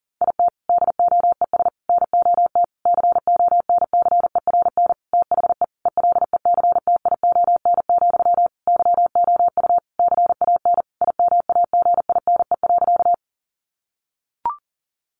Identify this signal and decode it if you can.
Morse code, heard as a keyed tone, about 40 wpm, 700 Hz. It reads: IT DOES NOT CONCERN THE ELECTION, YOU CAN IMAGINE.
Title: Sentences from Top 1500 Words 40wpm